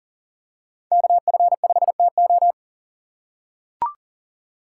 Morse code, heard as a keyed tone, about 40 wpm, 700 Hz. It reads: KF5TO
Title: USA Call Signs - 5 Characters - 6x Character Spacing - 40wpm